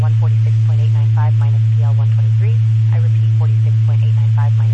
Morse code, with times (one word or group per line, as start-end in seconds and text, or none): none